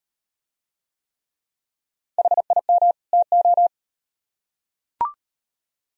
Morse code, heard as a keyed tone, about 38 wpm, 700 Hz. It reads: HIM TO